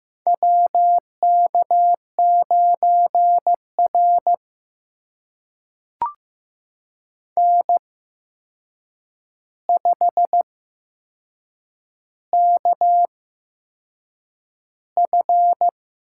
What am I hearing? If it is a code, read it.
Morse code, heard as a keyed tone, about 15 wpm, spaced as for 6 wpm, 700 Hz. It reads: WK9R N5KF
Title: USA Call Signs - 4 Characters - 8x Character Spacing - 15wpm